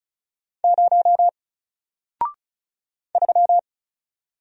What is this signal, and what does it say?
Morse code, heard as a keyed tone, about 36 wpm, 700 Hz. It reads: 0 3
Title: Single Letter-Number Alt 35wpm